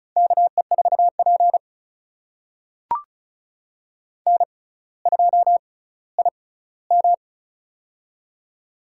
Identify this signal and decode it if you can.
Morse code, heard as a keyed tone, about 36 wpm, spaced as for 18 wpm, 690 Hz. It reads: KE4P N2IM
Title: USA Call Signs - 4 Characters - 6x Character Spacing - 35wpm